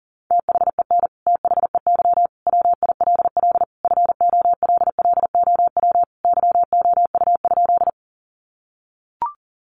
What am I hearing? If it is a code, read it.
Morse code, heard as a keyed tone, about 40 wpm, 700 Hz. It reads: THEN THEY WILL FOLLOW YOU?
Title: Sentences from Top 200 Words 40wpm